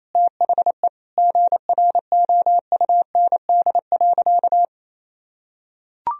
Morse code, heard as a keyed tone, about 27 wpm, 700 Hz. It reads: THE GROUND.
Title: Sentences from Top 1000 Words - Review Part 2 - 28wpm